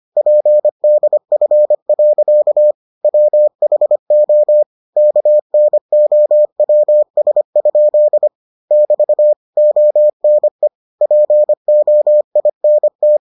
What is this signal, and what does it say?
Morse code, heard as a keyed tone, about 25 wpm, 600 Hz. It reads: PDF. WHO KNOWS? <BT> ONE POINT